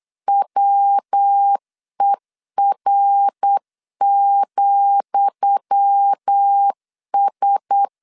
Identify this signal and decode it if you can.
Morse code, heard as a keyed tone, about 8 wpm, 800 Hz. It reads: WER,S
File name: LEZIONI / 10 SIGNS / 10D COMA 8wpm 3749